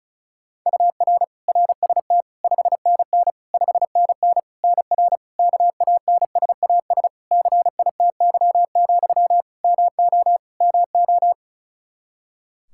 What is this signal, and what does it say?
Morse code, heard as a keyed tone, about 36 wpm, 700 Hz. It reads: UR RST 5NN 5NN NR KANSAS CITY, MO MO
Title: Practice QSO - Exchange 1 - Beginning - 35wpm